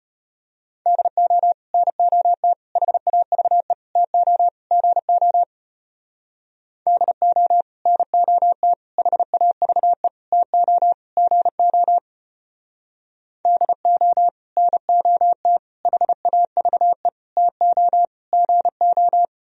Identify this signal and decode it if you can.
Morse code, heard as a keyed tone, about 30 wpm, 700 Hz. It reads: DO NOT HAVE TO GO DO NOT HAVE TO GO DO NOT HAVE TO GO
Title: Speed-Racing - Sets of 5 Words 25wpm